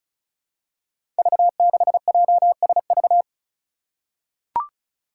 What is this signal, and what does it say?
Morse code, heard as a keyed tone, about 36 wpm, 700 Hz. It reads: V6JSV
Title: Five Challenging Letter-Number Combinations 35wpm